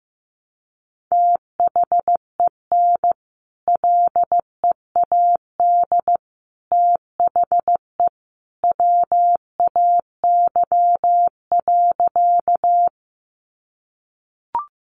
Morse code, heard as a keyed tone, about 15 wpm, 700 Hz. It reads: THEN LEAD THE WAY.
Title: Sentences from Top 500 Words 15wpm